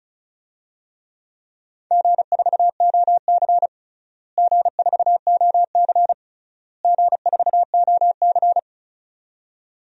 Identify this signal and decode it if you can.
Morse code, heard as a keyed tone, about 36 wpm, 700 Hz. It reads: G4OC G4OC G4OC
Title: DXCC-100 Call Signs - 4 Characters - Repeated 3x - 35wpm